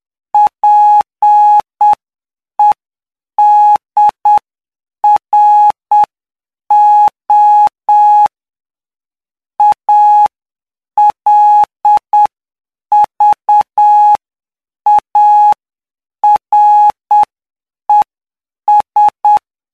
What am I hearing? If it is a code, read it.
Morse code, heard as a keyed tone, about 8 wpm, spaced as for 7 wpm, 840 Hz. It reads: PEDRO ALVARES
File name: cw / py2lcd / 14-PY2LCD-Li��o 23